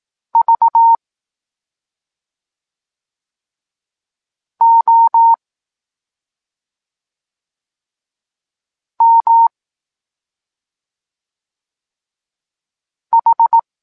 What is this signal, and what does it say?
Morse code, heard as a keyed tone, about 18 wpm, spaced as for 2 wpm, 920 Hz. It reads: VOMH
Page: Tekst 8 – doskonalenie odbioru poznanych liter